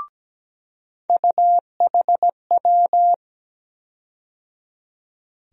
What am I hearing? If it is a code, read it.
Morse code, heard as a keyed tone, about 17 wpm, 700 Hz. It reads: UHW
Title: USA Call Sign Postfixes - 3 Letters 17wpm